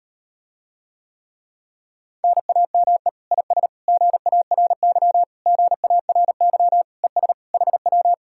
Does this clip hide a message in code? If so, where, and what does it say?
Yes: Morse code from 2.2 to 8.1 s, heard as a keyed tone, about 38 wpm, 700 Hz. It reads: NAME IS GARY GARY ES HW